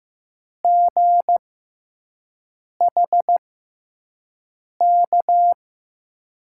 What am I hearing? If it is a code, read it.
Morse code, heard as a keyed tone, about 15 wpm, spaced as for 5 wpm, 700 Hz. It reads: GHK
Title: USA Call Signs - 6 Characters - 6x Character Spacing - 15wpm